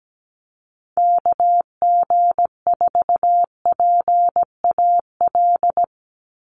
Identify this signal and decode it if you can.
Morse code, heard as a keyed tone, about 17 wpm, 700 Hz. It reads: KG4PAL
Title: USA Call Signs 17wpm